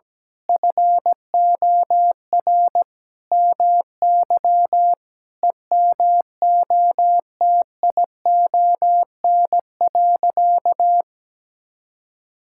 Morse code, heard as a keyed tone, about 17 wpm, 700 Hz. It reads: FOR MY EMOTION.
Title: Sentences from Top 1600 Words 17wpm